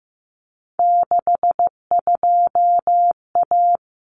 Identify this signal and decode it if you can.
Morse code, heard as a keyed tone, about 15 wpm, 700 Hz. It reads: 62A
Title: Common Cut Numbers - 3-Digit 15wpm